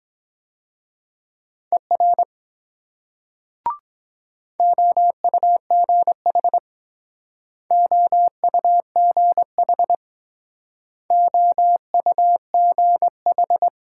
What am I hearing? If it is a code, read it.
Morse code, heard as a keyed tone, about 22 wpm, 700 Hz. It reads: ER OUGH OUGH OUGH